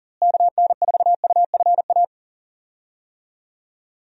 Morse code, heard as a keyed tone, about 40 wpm, 700 Hz. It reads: KN4UFA
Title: USA Call Signs - 6 Characters - Repeated 3x - 40wpm